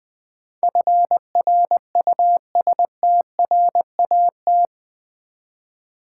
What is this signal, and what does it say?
Morse code, heard as a keyed tone, about 20 wpm, 700 Hz. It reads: FRUSTRAT
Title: Build A Word - 4 Letters At A Time - 11-Letter Words - 20wpm